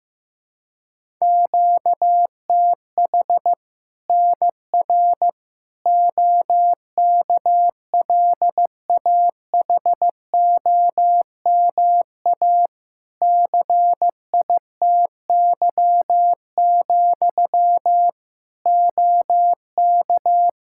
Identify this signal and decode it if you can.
Morse code, heard as a keyed tone, about 15 wpm, 700 Hz. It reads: QTH NR OKLAHOMA CITY, OK
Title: Practice QSO - Exchange 1 - Beginning - 15wpm